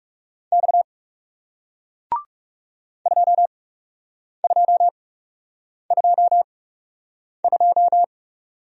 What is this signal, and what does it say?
Morse code, heard as a keyed tone, about 41 wpm, spaced as for 8 wpm, 700 Hz. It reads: X 2222